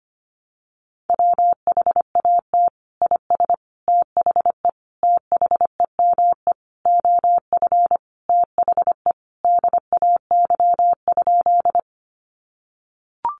Morse code, heard as a keyed tone, about 25 wpm, 700 Hz. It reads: WHAT IS THE THEME OF THE DAY?
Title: Sentences from Top 1100 Words 25wpm